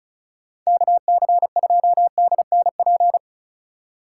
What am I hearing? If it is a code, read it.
Morse code, heard as a keyed tone, about 34 wpm, 700 Hz. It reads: KC2DNP
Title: USA Call Signs - 6 Characters - 2x Character Spacing - 35wpm